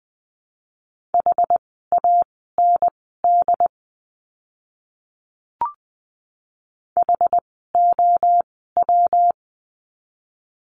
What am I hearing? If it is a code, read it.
Morse code, heard as a keyed tone, about 20 wpm, spaced as for 14 wpm, 700 Hz. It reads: HAND HOW
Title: 192 Lesson 31f - Letter v - Words 20wpm